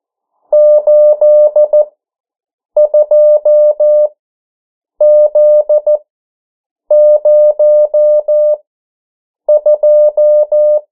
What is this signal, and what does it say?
Morse code, heard as a keyed tone, about 14 wpm, spaced as for 7 wpm, 600 Hz. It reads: 82Z02
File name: morse mp3 / 09 urok 8